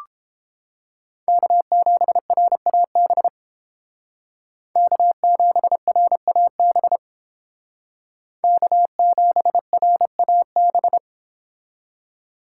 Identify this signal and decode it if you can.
Morse code, heard as a keyed tone, about 30 wpm, 700 Hz. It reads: K7RAB K7RAB K7RAB